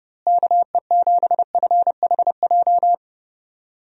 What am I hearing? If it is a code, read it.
Morse code, heard as a keyed tone, about 30 wpm, 700 Hz. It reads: KE7FHJ
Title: USA Call Signs - 6 Characters - Repeated 2x - 30wpm